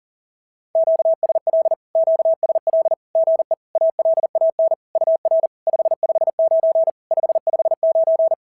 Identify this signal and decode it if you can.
Morse code, heard as a keyed tone, about 40 wpm, 650 Hz. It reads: QSL QSL GE ALAN UR 559 559
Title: SOTA QSO Practice 40wpm